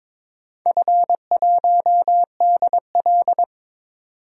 Morse code, heard as a keyed tone, about 22 wpm, 700 Hz. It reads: F1DL
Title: DXCC-100 Call Signs - 4 Characters - 22wpm